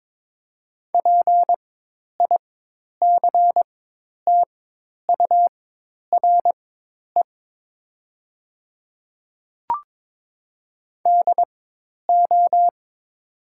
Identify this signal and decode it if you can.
Morse code, heard as a keyed tone, about 22 wpm, spaced as for 10 wpm, 700 Hz. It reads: PICTURE DO
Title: 4x Character Spacing - Top 200 Words - 22wpm